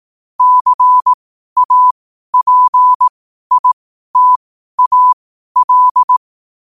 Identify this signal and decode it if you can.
Morse code, heard as a keyed tone, about 18 wpm, spaced as for 12 wpm, 1000 Hz. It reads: CAPITAL
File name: qotdfiles / 12wpm